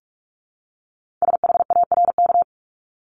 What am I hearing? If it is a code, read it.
Morse code, heard as a keyed tone, about 46 wpm, 700 Hz. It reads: SHARK
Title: Multi-letter Phonemes 45wpm